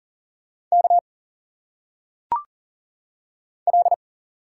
Morse code, heard as a keyed tone, about 40 wpm, 700 Hz. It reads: K L
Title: Single Letters - Phonetic Pronunciation 40wpm